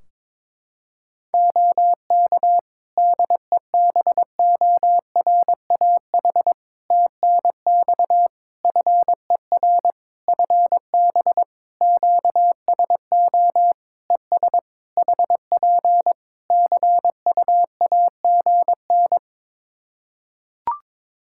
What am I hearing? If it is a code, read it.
Morse code, heard as a keyed tone, about 22 wpm, 700 Hz. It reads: OK DEBORAH TNX FER FB QSO ES HP CUAGN
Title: Practice QSO - Exchange 4 - Ending - 22wpm